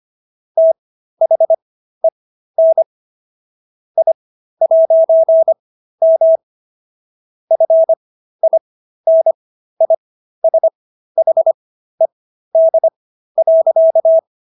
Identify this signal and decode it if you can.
Morse code, heard as a keyed tone, about 25 wpm, spaced as for 13 wpm, 650 Hz. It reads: THEN I'M FINISHED.
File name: SS106 13